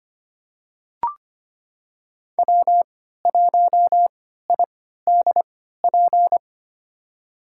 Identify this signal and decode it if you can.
Morse code, heard as a keyed tone, about 25 wpm, spaced as for 14 wpm, 700 Hz. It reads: W1IDP
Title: USA Call Signs - 5 Characters - 3x Character Spacing - 25wpm